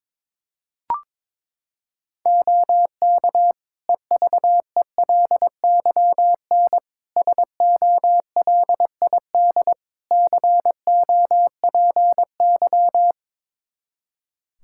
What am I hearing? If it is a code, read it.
Morse code, heard as a keyed tone, about 22 wpm, 700 Hz. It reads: OK EVELYN SOLID COPY
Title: Practice QSO - Exchange 3 - More Info - 22wpm